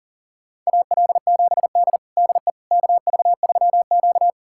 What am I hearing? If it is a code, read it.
Morse code, heard as a keyed tone, about 40 wpm, 700 Hz. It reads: AL7D DE KV3Q
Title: USA Callsign DE Callsign - 4 Characters - 40wpm